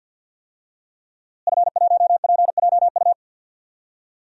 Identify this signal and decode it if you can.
Morse code, heard as a keyed tone, about 52 wpm, 700 Hz. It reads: F1PJU
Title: DXCC-100 Call Signs - 5 Characters - Repeated 2x - 50wpm